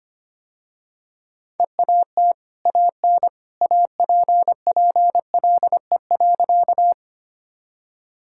Morse code, heard as a keyed tone, about 25 wpm, 700 Hz. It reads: EAT AN APPLE.
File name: Easy Sentences with Hints - 25wpm